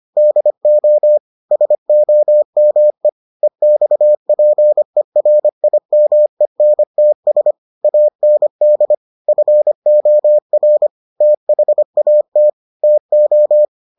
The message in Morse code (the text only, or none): DO SOME EXPERIMENTS AND FOR THAT TO